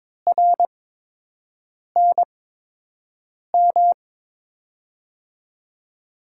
Morse code, heard as a keyed tone, about 22 wpm, spaced as for 6 wpm, 700 Hz. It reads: RNM
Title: USA Call Signs - 8x Character Spacing - 22wpm